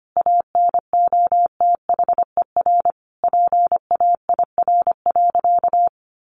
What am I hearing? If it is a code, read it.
Morse code, heard as a keyed tone, about 25 wpm, 700 Hz. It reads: ANOTHER PAIR.
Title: Sentences from Top 1200 Words 25wpm